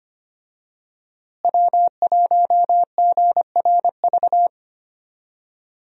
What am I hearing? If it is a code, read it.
Morse code, heard as a keyed tone, about 25 wpm, 700 Hz. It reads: W1GRV
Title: USA Call Signs - 5 Characters - Repeated 2x - 25wpm